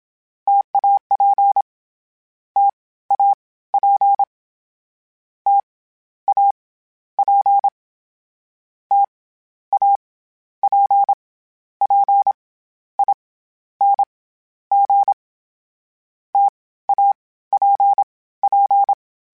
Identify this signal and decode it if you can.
Morse code, heard as a keyed tone, about 27 wpm, 800 Hz. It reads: TAP T A P T A P T A P P I N G T A P P
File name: Variable Spacing / Variable Spacing-mp3s / T AOEs @25